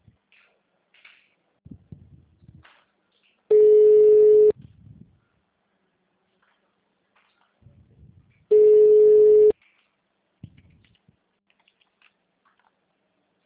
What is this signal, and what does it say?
Ringback tone (Europe, 425 Hz).